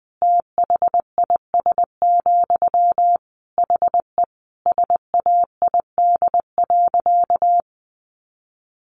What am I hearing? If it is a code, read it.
Morse code, heard as a keyed tone, about 19 wpm, 700 Hz. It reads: THIS, HE SAID.
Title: Sentences from Top 400 Words 20wpm